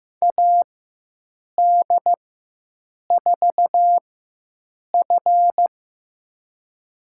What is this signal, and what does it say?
Morse code, heard as a keyed tone, about 15 wpm, spaced as for 7 wpm, 700 Hz. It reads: AD4F
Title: USA Call Signs - 4x Character Spacing - 15wpm